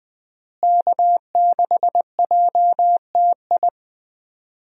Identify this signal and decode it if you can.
Morse code, heard as a keyed tone, about 20 wpm, 700 Hz. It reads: K6JTI
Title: USA Call Signs - 5 Characters - Repeated 3x - 20wpm